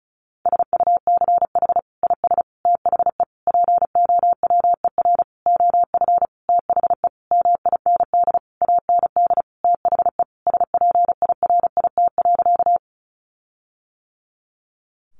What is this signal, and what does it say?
Morse code, heard as a keyed tone, about 34 wpm, 700 Hz. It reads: SUCH IS THE POWER OF THE MIND AND THE SPIRIT.
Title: Sentences from Top 1400 Words 35wpm